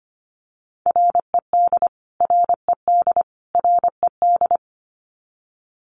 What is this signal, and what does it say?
Morse code, heard as a keyed tone, about 25 wpm, 700 Hz. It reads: RED RED RED
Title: Top 500 Words Repeated 25wpm